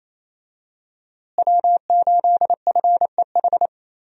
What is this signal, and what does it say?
Morse code, heard as a keyed tone, about 28 wpm, 700 Hz. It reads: W8FEH
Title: USA Call Signs - 5 Characters - Repeated 2x - 28wpm